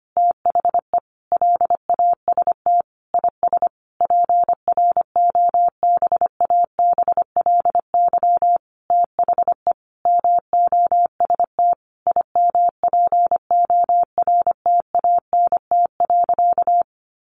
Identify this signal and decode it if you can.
Morse code, heard as a keyed tone, about 25 wpm, 700 Hz. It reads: THE LAST IS PROBABLY THE MOST IMPORTANT.
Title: Sentences from Top 700 Words 25wpm